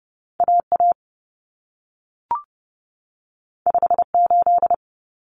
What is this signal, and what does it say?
Morse code, heard as a keyed tone, about 30 wpm, 700 Hz. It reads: AA 58
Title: Common Cut Numbers - 2-Digit 30wpm